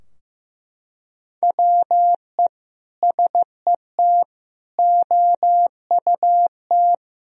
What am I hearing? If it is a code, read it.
Morse code, heard as a keyed tone, about 15 wpm, 700 Hz. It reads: WE SET OUT
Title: Sentences from Top 1000 Words - Review Part 2 - 15wpm